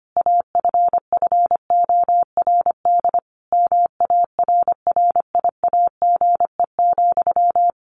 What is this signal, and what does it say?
Morse code, heard as a keyed tone, about 25 wpm, 700 Hz. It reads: AFFORD MARRIAGE,
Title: Sentences from Top 1100 Words 25wpm